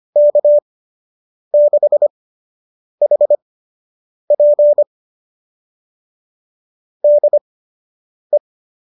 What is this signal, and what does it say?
Morse code, heard as a keyed tone, about 25 wpm, spaced as for 8 wpm, 590 Hz. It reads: K6HP DE
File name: qso003 08